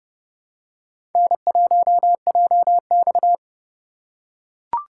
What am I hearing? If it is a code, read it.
Morse code, heard as a keyed tone, about 30 wpm, 700 Hz. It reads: N1JX
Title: USA Call Signs - 4 Characters - 2x Character Spacing - 30wpm